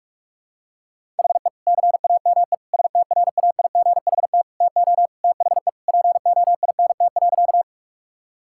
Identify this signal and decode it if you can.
Morse code, heard as a keyed tone, about 46 wpm, 700 Hz. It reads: HE CAME STRAIGHT TO THE POINT.
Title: Sentences from Top 1000 Words - Review Part 6 - 45wpm